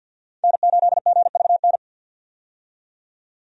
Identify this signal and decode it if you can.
Morse code, heard as a keyed tone, about 51 wpm, 700 Hz. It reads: N8GVN
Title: USA Call Signs - 5 Characters - 50wpm